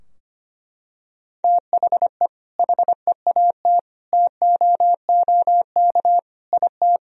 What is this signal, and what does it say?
Morse code, heard as a keyed tone, about 25 wpm, 700 Hz. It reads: THE HEAT TOOK IT
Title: Sentences from Top 1000 Words - Review Part 3 - 25wpm